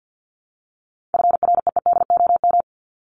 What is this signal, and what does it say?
Morse code, heard as a keyed tone, about 50 wpm, 700 Hz. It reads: FREEDOM
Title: Top 501-750 Words 50wpm